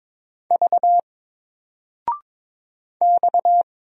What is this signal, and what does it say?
Morse code, heard as a keyed tone, about 22 wpm, 700 Hz. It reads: V X
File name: Single-Letter-Number-22wpm